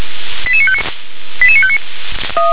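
Five-tone selective call (ZVEI), heard as quick runs of five tones, tones of about 70 ms. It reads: 88959 88959
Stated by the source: ZVEI five-tone selective call received on SDR.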